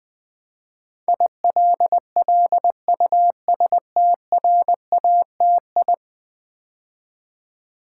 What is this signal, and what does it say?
Morse code, heard as a keyed tone, about 20 wpm, 700 Hz. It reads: ILLUSTRATI
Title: Build A Word - 5 Letters At A Time - 12-Letter Words - 20wpm